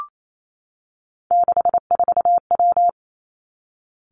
Morse code, heard as a keyed tone, about 28 wpm, 700 Hz. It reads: 64W
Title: Three Challenging Letter-Number Combinations 28wpm